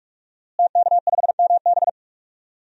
Morse code, heard as a keyed tone, about 46 wpm, 700 Hz. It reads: TK5MB
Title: DXCC-100 Call Signs - 5 Characters - Repeated 3x - 45wpm